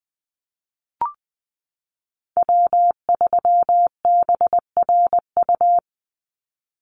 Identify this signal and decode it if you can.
Morse code, heard as a keyed tone, about 20 wpm, 700 Hz. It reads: W3BRU
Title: USA Call Signs 20wpm